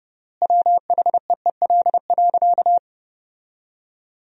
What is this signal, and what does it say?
Morse code, heard as a keyed tone, about 30 wpm, 700 Hz. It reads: WHEEL.
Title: 3x Word-Spacing - Sentences from Top 500 Words - 30wpm